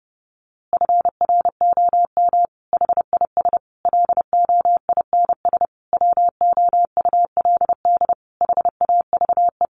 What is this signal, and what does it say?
Morse code, heard as a keyed tone, about 30 wpm, 700 Hz. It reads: FROM HIS LOINS WOULD HAVE
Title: A Sound Of Thunder 30wpm